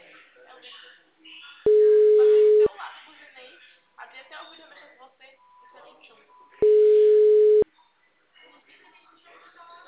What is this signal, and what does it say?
Ringback tone (Europe, 425 Hz), with speech.
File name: out- -2011- - - .100